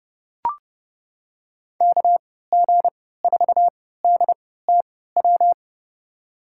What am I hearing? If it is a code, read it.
Morse code, heard as a keyed tone, about 30 wpm, spaced as for 17 wpm, 700 Hz. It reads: KG4DTW
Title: USA Call Signs - 3x Character Spacing - 30wpm